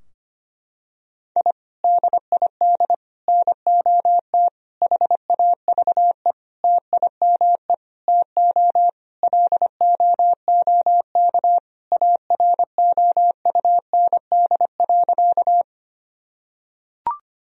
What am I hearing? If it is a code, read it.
Morse code, heard as a keyed tone, about 25 wpm, 700 Hz. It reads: I DID NOT HAVE TIME TO LOOK AROUND.
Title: Sentences from Top 1000 Words - Review Part 3 - 25wpm